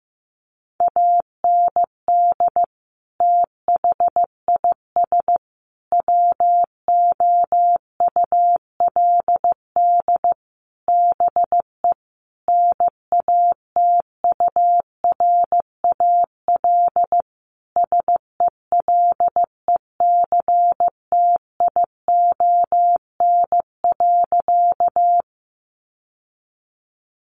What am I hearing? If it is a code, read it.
Morse code, heard as a keyed tone, about 15 wpm, 700 Hz. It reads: AND THIS WOULD BE NATURAL SELECTION.
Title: Sentences from Top 1400 Words 15wpm